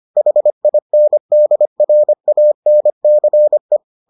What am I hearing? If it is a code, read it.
Morse code, heard as a keyed tone, about 25 wpm, 600 Hz. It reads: HINDRANCE